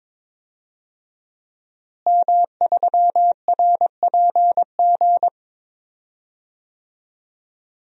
Morse code, heard as a keyed tone, about 22 wpm, 700 Hz. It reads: M3RPG
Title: DXCC-100 Call Signs - 5 Characters - Repeated 2x - 22wpm